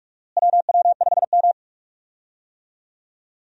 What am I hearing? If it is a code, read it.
Morse code, heard as a keyed tone, about 46 wpm, 700 Hz. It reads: WW5M
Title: USA Call Signs - 4 Characters - Repeated 2x - 45wpm